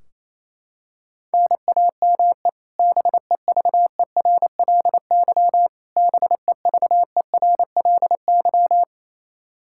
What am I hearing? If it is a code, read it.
Morse code, heard as a keyed tone, about 28 wpm, 700 Hz. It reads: NAME BEVERLY BEVERLY
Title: Practice QSO - Exchange 1 - Beginning - 28wpm